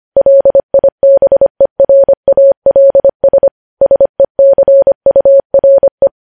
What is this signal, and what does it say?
Morse code, heard as a keyed tone, about 25 wpm, 550 Hz. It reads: LIBERALS SECURE
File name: podcast / News headlines-1x-25x25